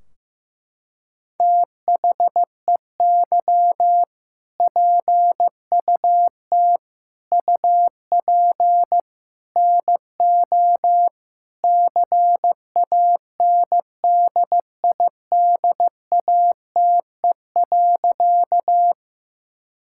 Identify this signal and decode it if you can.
Morse code, heard as a keyed tone, about 15 wpm, 700 Hz. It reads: THEY PUT UP NO CANDIDATE.
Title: Sentences from Top 1600 Words 15wpm